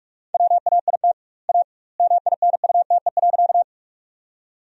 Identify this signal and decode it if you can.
Morse code, heard as a keyed tone, about 48 wpm, 700 Hz. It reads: WAIT A MINUTE.
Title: 2x Word-Spacing - Sentences from Top 500 Words - 45wpm